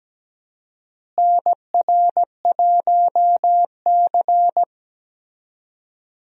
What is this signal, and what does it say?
Morse code, heard as a keyed tone, about 17 wpm, 700 Hz. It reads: NR1C
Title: USA Call Signs - 4 Characters - Repeated 3x - 17wpm